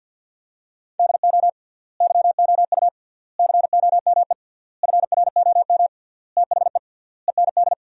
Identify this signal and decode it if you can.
Morse code, heard as a keyed tone, about 50 wpm, 700 Hz. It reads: DO YOU COME FROM THE END
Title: 3x Word-Spacing - Sentences from Top 200 Words - 50wpm